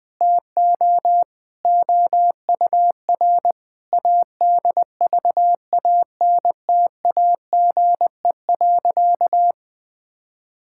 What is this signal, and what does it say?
Morse code, heard as a keyed tone, about 21 wpm, 700 Hz. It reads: TO OUR ADVANTAGE.
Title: Sentences from Top 1000 Words - Review Part 6 - 20wpm